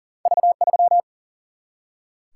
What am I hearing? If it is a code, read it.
Morse code, heard as a keyed tone, about 40 wpm, 700 Hz. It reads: V 3
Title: Single-Letter Single-Number 40wpm